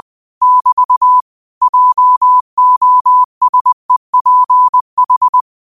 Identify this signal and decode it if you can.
Morse code, heard as a keyed tone, about 20 wpm, 1000 Hz. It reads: <BT> JOSEPH
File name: 20wpm